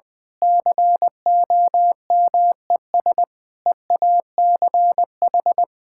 Morse code, heard as a keyed tone, about 20 wpm, 700 Hz. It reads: COMES EACH
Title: Easy Sentences with Hints - 20wpm